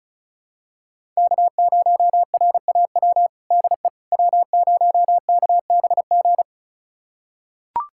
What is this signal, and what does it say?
Morse code, heard as a keyed tone, about 34 wpm, 700 Hz. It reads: K0RAW DE W0KBG
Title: USA Callsign DE Callsign - 5 Characters - 35wpm